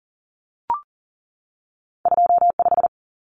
Morse code, heard as a keyed tone, about 40 wpm, 700 Hz. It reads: 25
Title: Common Cut Numbers - 2-Digit 40wpm